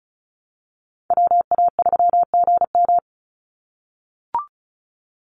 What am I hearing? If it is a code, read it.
Morse code, heard as a keyed tone, about 34 wpm, 700 Hz. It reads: WA3GM
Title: Call Signs - 35wpm